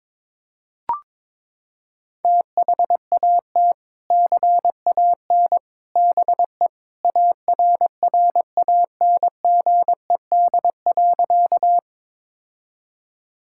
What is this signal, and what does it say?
Morse code, heard as a keyed tone, about 22 wpm, 700 Hz. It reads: THAT CAN BE ARRANGED.
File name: practice / 22wpm / Sentences with 3-Word Phrases - 22wpm